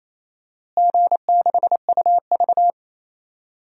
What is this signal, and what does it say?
Morse code, heard as a keyed tone, about 28 wpm, 700 Hz. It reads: G6UV
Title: DXCC-100 Call Signs - 4 Characters - Repeated 3x - 28wpm